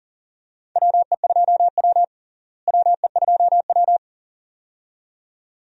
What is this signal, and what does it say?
Morse code, heard as a keyed tone, about 40 wpm, 700 Hz. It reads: WE2W WE2W
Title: USA Call Signs - 4 Characters - Repeated 2x - 40wpm